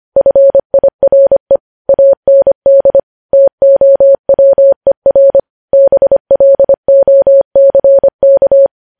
Morse code, heard as a keyed tone, about 25 wpm, 550 Hz. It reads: FIRE AND TOWER BLOCK
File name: podcast / News headlines-1x-25x25